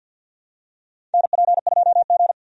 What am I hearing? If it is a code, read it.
Morse code, heard as a keyed tone, about 52 wpm, 700 Hz. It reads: NP2G
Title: USA Call Signs - 4 Characters - 50wpm